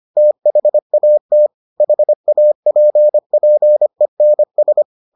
Morse code, heard as a keyed tone, about 25 wpm, 600 Hz. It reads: THAT HAPPENS